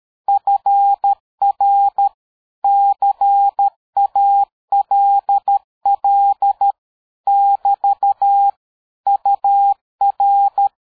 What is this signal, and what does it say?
Morse code, heard as a keyed tone, about 13 wpm, 800 Hz. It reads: FR CALL <BT> UR
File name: dk5ke / QSO 60BpM